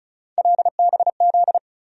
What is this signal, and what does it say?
Morse code, heard as a keyed tone, about 36 wpm, 700 Hz. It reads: LBZ
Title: USA Call Sign Postfixes - 3 Letters 35wpm